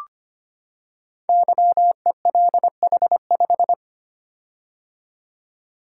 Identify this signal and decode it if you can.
Morse code, heard as a keyed tone, about 25 wpm, 700 Hz. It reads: YELH5
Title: Code Groups - 5 Letter:Numbers - 25wpm